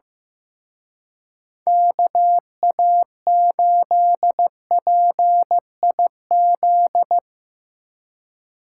Morse code, heard as a keyed tone, about 16 wpm, 700 Hz. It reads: KA8PIZ
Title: Speed Racing - USA 2x3 Call Signs 15wpm